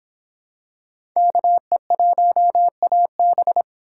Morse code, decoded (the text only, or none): KE1AB